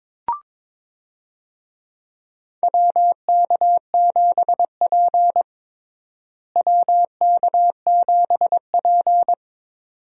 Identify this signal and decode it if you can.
Morse code, heard as a keyed tone, about 22 wpm, 700 Hz. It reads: WK7P WK7P
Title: USA Call Signs - 4 Characters - Repeated 2x - 22wpm